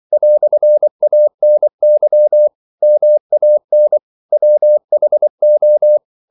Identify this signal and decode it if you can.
Morse code, heard as a keyed tone, about 24 wpm, 600 Hz. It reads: "ANY MAN WHO
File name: CW / Practice / EnglishQuotations0224